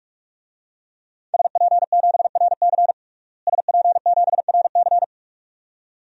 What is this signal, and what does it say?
Morse code, heard as a keyed tone, about 46 wpm, 700 Hz. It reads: SP7RC SP7RC
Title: DXCC-100 Call Signs - 5 Characters - Repeated 2x - 45wpm